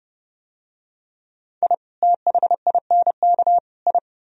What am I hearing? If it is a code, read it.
Morse code, heard as a keyed tone, about 29 wpm, 700 Hz. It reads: I THINK I